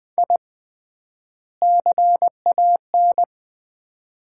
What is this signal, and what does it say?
Morse code, heard as a keyed tone, about 20 wpm, 700 Hz. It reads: I CAN
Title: 3x Word-Spacing - Sets of 2 Words - 20wpm